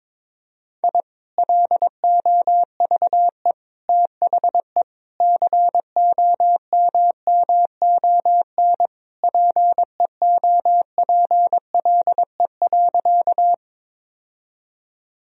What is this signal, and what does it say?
Morse code, heard as a keyed tone, about 22 wpm, 700 Hz. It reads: I LOVE THE COMMON PEOPLE.
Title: Sentences from Top 1000 Words - Review Part 3 - 22wpm